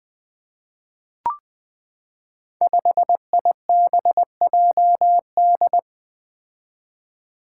Morse code, heard as a keyed tone, about 20 wpm, 700 Hz. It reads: 5IBJD
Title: Five Challenging Letter-Number Combinations 20wpm